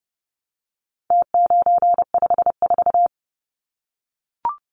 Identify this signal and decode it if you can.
Morse code, heard as a keyed tone, about 30 wpm, 700 Hz. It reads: T954
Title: Numbers - Contest Style - 4-Digit 30wpm